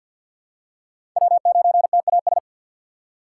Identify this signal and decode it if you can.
Morse code, heard as a keyed tone, about 47 wpm, 690 Hz. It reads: W9TAS
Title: USA Call Signs - 5 Characters - 3x Character Spacing - 50wpm